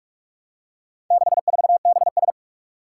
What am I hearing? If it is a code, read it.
Morse code, heard as a keyed tone, about 46 wpm, 700 Hz. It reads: 64BS
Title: Four Challenging Letter-Number Combinations 45wpm